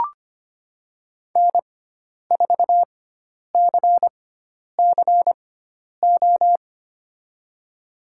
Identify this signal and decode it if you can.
Morse code, heard as a keyed tone, about 25 wpm, spaced as for 10 wpm, 700 Hz. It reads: N4CCO
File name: Lesson 16g - Letter c - Call Signs 25wpm